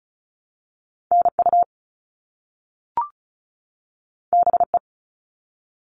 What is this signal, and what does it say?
Morse code, heard as a keyed tone, about 34 wpm, 700 Hz. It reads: NU BE
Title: All Cut Numbers - 2-Digit 35wpm